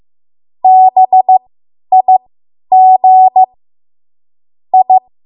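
Morse code, heard as a keyed tone, about 15 wpm, spaced as for 10 wpm, 750 Hz. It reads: BIG I
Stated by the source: Morse code practice run at 10 WPM from a code practice archive.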